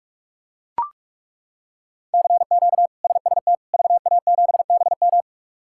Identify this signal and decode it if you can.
Morse code, heard as a keyed tone, about 41 wpm, 700 Hz. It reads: CQ SST VA7BM
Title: K1USN SST QSO Practice - 45wpm